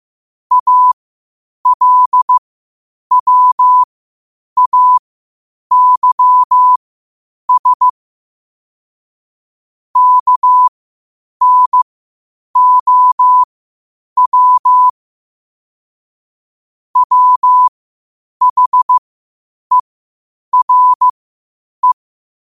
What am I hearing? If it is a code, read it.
Morse code, heard as a keyed tone, about 15 wpm, spaced as for 8 wpm, 1000 Hz. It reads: ALWAYS KNOW WHERE
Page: New quotes every day in morse code at 8 Words per minute.